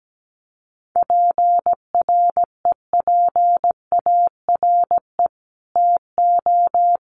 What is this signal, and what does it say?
Morse code, heard as a keyed tone, about 17 wpm, 700 Hz. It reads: PREPARE TO
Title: Sentences from Top 900 Words 17wpm